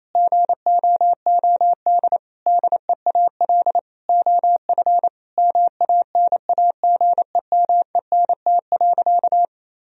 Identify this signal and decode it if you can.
Morse code, heard as a keyed tone, about 28 wpm, 700 Hz. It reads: GOOD DEAL OF MANAGEMENT.
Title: Sentences from Top 1000 Words - Review Part 4 - 28wpm